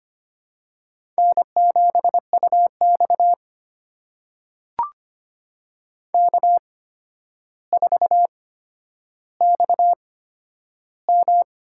Morse code, heard as a keyed tone, about 25 wpm, spaced as for 11 wpm, 700 Hz. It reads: N7UX K4XM
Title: USA Call Signs - 4 Characters - 8x Character Spacing - 25wpm